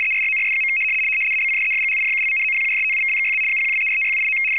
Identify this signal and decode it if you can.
RTTY, about 100 baud. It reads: ERRORS. ABCDEFGHIJKLMNOPQRSTUVWXYZ 0123456789
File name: w ry425 / rtty100.wav / rtty100